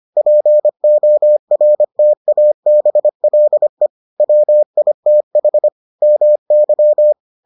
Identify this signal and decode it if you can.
Morse code, heard as a keyed tone, about 25 wpm, 600 Hz. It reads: PORTABLE WITH MY